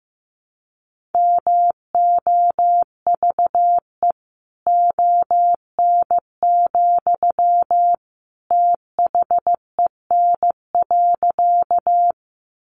Morse code, heard as a keyed tone, about 15 wpm, 700 Hz. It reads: MOVE ON, THEN.
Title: Sentences from Top 200 Words 15wpm